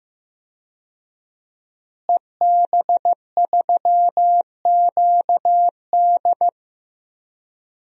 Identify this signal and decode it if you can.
Morse code, heard as a keyed tone, about 16 wpm, 700 Hz. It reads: EB3QD
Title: DXCC-100 Call Signs - 5 Characters - Repeated 2x - 15wpm